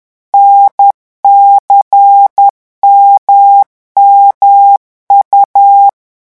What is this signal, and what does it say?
Morse code, heard as a keyed tone, about 10 wpm, 800 Hz. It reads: NCMMU
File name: morse mp3 / revision3-vitesse 10 mots